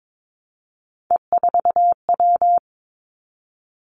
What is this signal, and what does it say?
Morse code, heard as a keyed tone, about 22 wpm, 700 Hz. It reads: E4W
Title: Three Challenging Letter-Number Combinations 22wpm